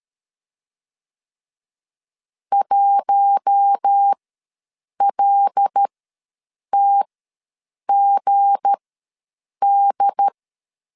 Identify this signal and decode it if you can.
Morse code, heard as a keyed tone, about 13 wpm, spaced as for 7 wpm, 800 Hz. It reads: 1LTGD
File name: LEZIONI / 8 YJ45 / 8 YJ4 ws 12wpm 3249